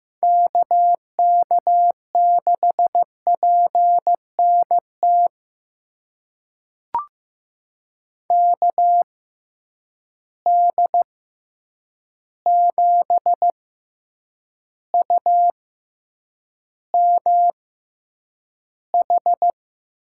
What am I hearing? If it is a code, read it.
Morse code, heard as a keyed tone, about 15 wpm, spaced as for 8 wpm, 700 Hz. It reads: KK6PNT KD7UMH
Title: USA Call Signs - 6 Characters - 6x Character Spacing - 15wpm